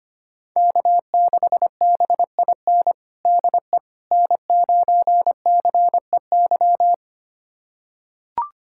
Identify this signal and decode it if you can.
Morse code, heard as a keyed tone, about 25 wpm, 700 Hz. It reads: K6BIN DE N9CEY
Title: USA Callsign DE Callsign - 5 Characters - 25wpm